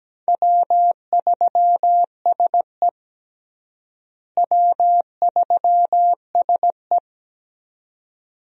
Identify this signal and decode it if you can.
Morse code, heard as a keyed tone, about 17 wpm, 700 Hz. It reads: W3SE W3SE
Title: USA Call Signs - 4 Characters - Repeated 2x - 17wpm